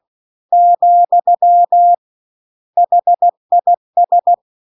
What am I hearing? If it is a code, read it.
Morse code, heard as a keyed tone, about 16 wpm, 700 Hz. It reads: , HIS
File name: CW-The-Big-Twoheaded-River-1-16WPM